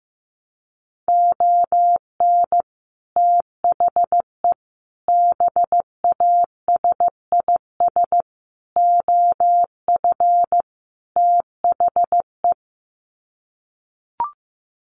Morse code, heard as a keyed tone, about 15 wpm, 700 Hz. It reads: ON THE BASIS OF THE
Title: Sets of 5 Words 15wpm